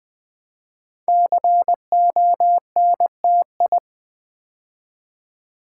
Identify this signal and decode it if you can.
Morse code, heard as a keyed tone, about 20 wpm, 700 Hz. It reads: CONTI
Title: Build A Word - 5 Letters At A Time - 10-Letter Words - 20wpm